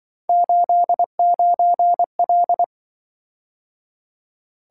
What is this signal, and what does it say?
Morse code, heard as a keyed tone, about 24 wpm, 700 Hz. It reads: 89L
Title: Speed-Racing - Three Letter-Number Combinations - 20wpm